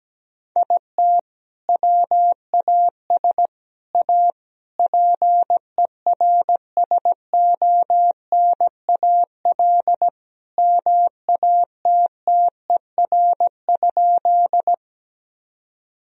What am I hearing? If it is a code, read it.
Morse code, heard as a keyed tone, about 17 wpm, 700 Hz. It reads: IT WAS A PERSONAL MATTER?
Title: Sentences from Top 1000 Words - Review Part 4 - 17wpm